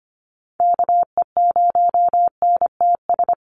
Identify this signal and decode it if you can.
Morse code, heard as a keyed tone, about 25 wpm, 700 Hz. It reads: KE0NTS
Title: USA Call Signs 25wpm